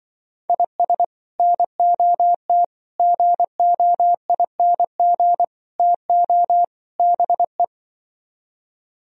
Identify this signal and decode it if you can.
Morse code, heard as a keyed tone, about 24 wpm, 700 Hz. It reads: IS NOT GOING TO BE
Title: Speed-Racing - Sets of 5 Words 20wpm